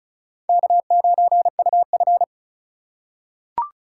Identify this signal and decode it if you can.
Morse code, heard as a keyed tone, about 34 wpm, 700 Hz. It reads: K9UF
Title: USA Call Signs - 2x Character Spacing - 35wpm